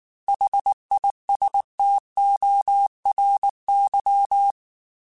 Morse code, heard as a keyed tone, about 19 wpm, 800 Hz. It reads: HISTORY
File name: T then H Runs / T then H Runs.mp3 / T then H Runs @18